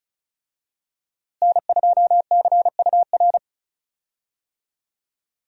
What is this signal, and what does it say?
Morse code, heard as a keyed tone, about 34 wpm, 700 Hz. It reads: N2CUR
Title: USA Call Signs - 5 Characters - Repeated 3x - 35wpm